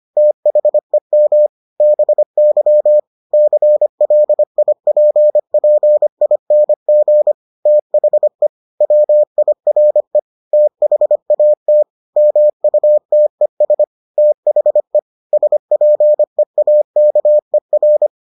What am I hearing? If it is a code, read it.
Morse code, heard as a keyed tone, about 25 wpm, 600 Hz. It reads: THEM BY CLIPPING THE WIRE THAT MUTES THE SPEAKER